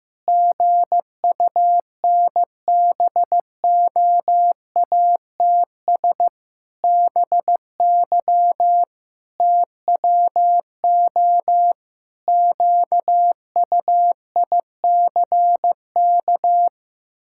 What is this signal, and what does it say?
Morse code, heard as a keyed tone, about 16 wpm, 700 Hz. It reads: GUNBOATS BY TWO QUICK
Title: Pangram Challenge v2 - 15wpm